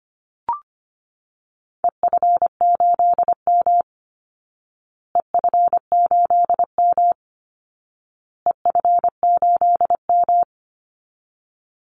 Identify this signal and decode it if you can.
Morse code, heard as a keyed tone, about 25 wpm, 700 Hz. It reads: EF8M EF8M EF8M
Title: Call Signs Repeated 2x - 25wpm